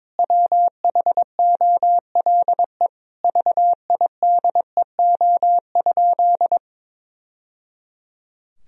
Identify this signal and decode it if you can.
Morse code, heard as a keyed tone, about 22 wpm, 700 Hz. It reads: WHOLE VIDEO?
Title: Sentences from Top 1000 Words - Review Part 4 - 22wpm